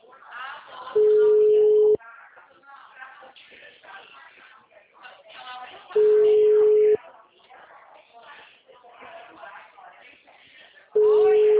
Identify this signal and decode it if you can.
Ringback tone (Europe, 425 Hz), with speech.